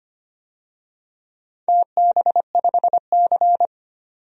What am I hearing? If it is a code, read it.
Morse code, heard as a keyed tone, about 25 wpm, 700 Hz. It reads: TB5C
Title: DXCC-100 Call Signs - 4 Characters - 25wpm